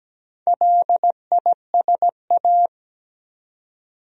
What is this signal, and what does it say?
Morse code, heard as a keyed tone, about 17 wpm, 700 Hz. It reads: LISA
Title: USA Common Names - 17wpm